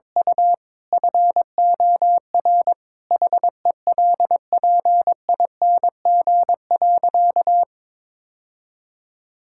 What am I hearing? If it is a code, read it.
Morse code, heard as a keyed tone, about 22 wpm, 700 Hz. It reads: U FOR HELPING.
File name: practice / 22wpm / Sentences with 3-Word Phrases - 22wpm